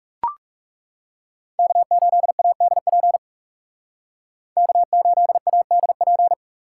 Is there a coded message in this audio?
Yes: Morse code from 1.6 to 6.3 s, heard as a keyed tone, about 38 wpm, 700 Hz. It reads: K8ADP K8ADP